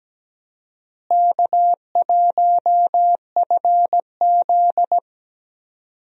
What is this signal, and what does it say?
Morse code, heard as a keyed tone, about 17 wpm, 700 Hz. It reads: K1FZ
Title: USA Call Signs - 4 Characters - Repeated 3x - 17wpm